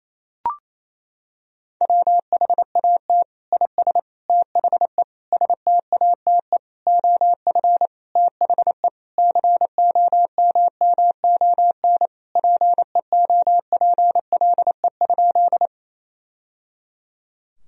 Morse code, heard as a keyed tone, about 28 wpm, 700 Hz. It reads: WHAT IS THE STATE OF THE COMMON PEOPLE?
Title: Sentences from Top 1000 Words - Review Part 3 - 28wpm